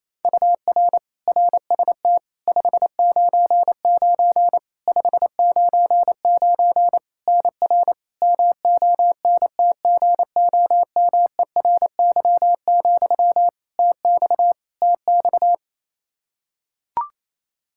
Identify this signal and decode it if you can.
Morse code, heard as a keyed tone, about 28 wpm, 700 Hz. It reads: UR RST 599 599 NR MONTGOMERY, TX TX
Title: Practice QSO - Exchange 1 - Beginning - 28wpm